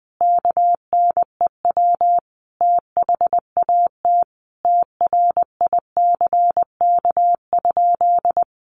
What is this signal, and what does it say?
Morse code, heard as a keyed tone, about 20 wpm, 700 Hz. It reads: KNEW THAT TRICK?
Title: Sentences from Top 1500 Words 20wpm